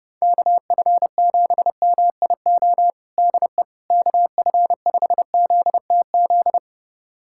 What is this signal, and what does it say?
Morse code, heard as a keyed tone, about 30 wpm, 700 Hz. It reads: KF7MIO DE KF5ZTZ
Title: USA Callsign DE Callsign 30wpm